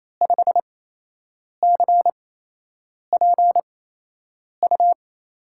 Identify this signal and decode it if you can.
Morse code, heard as a keyed tone, about 28 wpm, spaced as for 8 wpm, 700 Hz. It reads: 5CPU
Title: USA Call Signs - 5 Characters - 8x Character Spacing - 28wpm